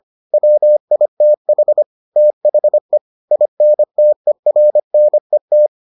Morse code, heard as a keyed tone, about 25 wpm, 600 Hz. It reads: WITH THE INTERNET